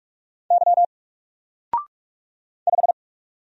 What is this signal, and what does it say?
Morse code, heard as a keyed tone, about 46 wpm, 700 Hz. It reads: Y 5
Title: Single Character (Letter:Number:Punctuation:Prosign) 45wpm